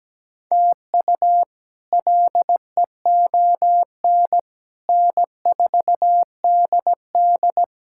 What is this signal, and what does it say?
Morse code, heard as a keyed tone, about 17 wpm, 700 Hz. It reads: TU LEON N4DD
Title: K1USN SST QSO Practice - 17wpm